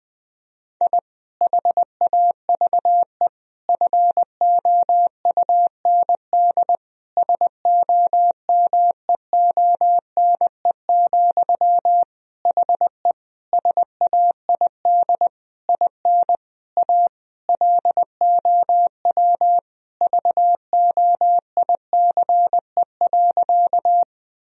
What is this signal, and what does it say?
Morse code, heard as a keyed tone, about 20 wpm, 700 Hz. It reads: I HAVE FOUND SOMEONE, HE SAID IN A LOW VOICE.
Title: Sentences from Top 1000 Words - Review Part 1 - 20wpm